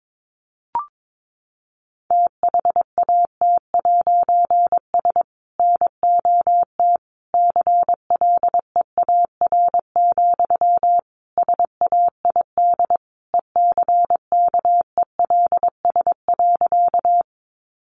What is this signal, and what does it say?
Morse code, heard as a keyed tone, about 22 wpm, 700 Hz. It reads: THAT'S NOT CLEAR, SAID ECKELS.
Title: A Sound Of Thunder 22wpm